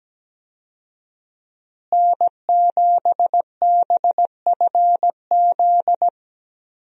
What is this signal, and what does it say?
Morse code, heard as a keyed tone, about 17 wpm, 700 Hz. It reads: N7BFZ
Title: USA Call Signs - 5 Characters - Repeated 2x - 17wpm